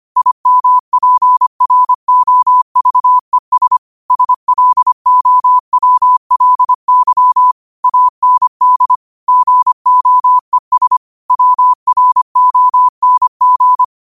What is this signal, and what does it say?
Morse code, heard as a keyed tone, about 25 wpm, 1000 Hz. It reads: IMPROVES SLOWLY AND GOES WRONG